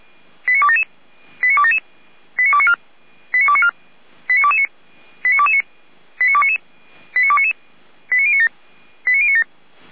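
Five-tone selective call (ZVEI), heard as quick runs of five tones, tones of about 70 ms. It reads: 89277 89277 89294 89294 89229 89229 89299 89299 89097 89097
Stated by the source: ZVEI five-tone selective call received with an SDR.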